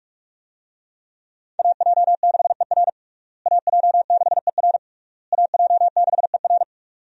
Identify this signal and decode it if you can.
Morse code, heard as a keyed tone, about 41 wpm, 700 Hz. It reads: AJ6ER AJ6ER AJ6ER
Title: USA Call Signs - 5 Characters - Repeated 3x - 45wpm